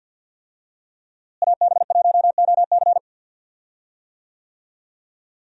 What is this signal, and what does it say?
Morse code, heard as a keyed tone, about 49 wpm, 700 Hz. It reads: AB1OC
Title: USA Call Signs - 5 Characters - Repeated 3x - 50wpm